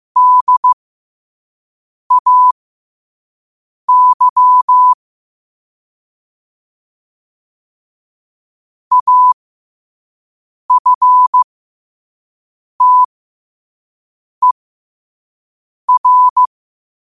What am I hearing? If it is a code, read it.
Morse code, heard as a keyed tone, about 15 wpm, spaced as for 5 wpm, 1000 Hz. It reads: DAY AFTER
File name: qotdfiles / 5wpm